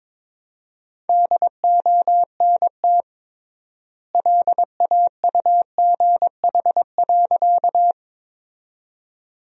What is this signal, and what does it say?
Morse code, heard as a keyed tone, about 22 wpm, 700 Hz. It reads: DONT LAUGH.
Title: 3x Word-Spacing - Sentences from Top 500 Words - 22wpm